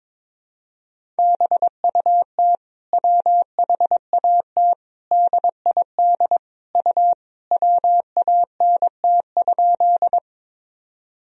Morse code, heard as a keyed tone, about 21 wpm, 700 Hz. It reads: BUT WHAT DID U WANT?
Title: Sentences from Top 200 Words and Abbreviations 22wpm